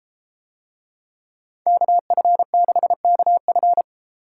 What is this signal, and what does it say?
Morse code, heard as a keyed tone, about 33 wpm, 700 Hz. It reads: KF6KF